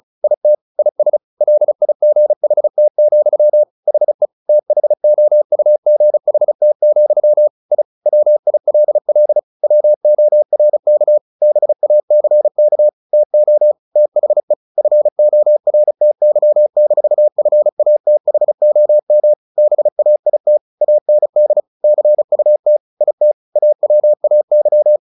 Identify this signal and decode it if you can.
Morse code, heard as a keyed tone, about 34 wpm, 600 Hz. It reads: IT IS LIGHT, HE THOUGHT, I WILL WORK BACK TO THE FORTY-FATHOM BAIT AND CUT IT AWAY
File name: the old man and the sea / the old man and the sea 1 35wpm